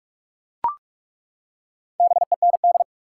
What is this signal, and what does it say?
Morse code, heard as a keyed tone, about 41 wpm, 700 Hz. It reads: BEND
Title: Four Letter Words - 45wpm